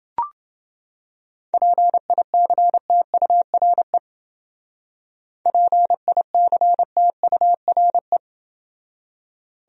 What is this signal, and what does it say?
Morse code, heard as a keyed tone, about 29 wpm, 700 Hz. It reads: PICTURE PICTURE